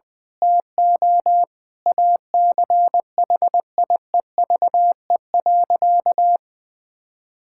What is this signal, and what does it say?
Morse code, heard as a keyed tone, about 21 wpm, 700 Hz. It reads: TO ACHIEVE.
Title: Sentences from Top 1000 Words - Review Part 6 - 20wpm